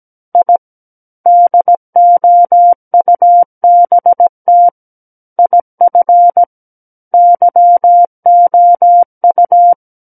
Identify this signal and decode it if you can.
Morse code, heard as a keyed tone, about 17 wpm, 700 Hz. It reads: I DOUBT IF YOU
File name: Book One / 08-Chapter 8 - 17 WPM